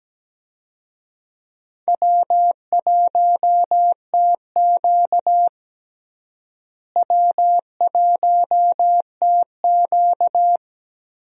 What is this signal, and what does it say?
Morse code, heard as a keyed tone, about 17 wpm, 700 Hz. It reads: W1TQ W1TQ
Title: USA Call Signs - 4 Characters - Repeated 2x - 17wpm